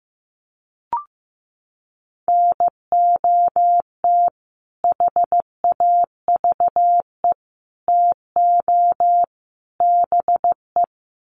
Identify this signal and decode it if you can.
Morse code, heard as a keyed tone, about 14 wpm, 700 Hz. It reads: NOT HAVE TO BE
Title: Sets of 4 Words 15wpm